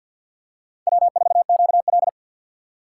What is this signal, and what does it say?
Morse code, heard as a keyed tone, about 52 wpm, 700 Hz. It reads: W4QL
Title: USA Call Signs - 4 Characters - 2x Character Spacing - 50wpm